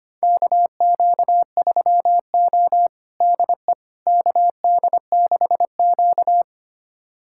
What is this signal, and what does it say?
Morse code, heard as a keyed tone, about 25 wpm, 700 Hz. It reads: KQ3O DE KD6Q
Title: USA Callsign DE Callsign - 4 Characters - 25wpm